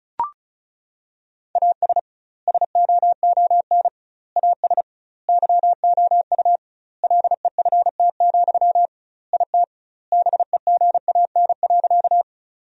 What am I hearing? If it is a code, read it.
Morse code, heard as a keyed tone, about 36 wpm, 700 Hz. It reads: AS SOON AS YOU LEFT, IT BEGAN.
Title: 2x Word-Spacing - Sentences from Top 400 Words - 35wpm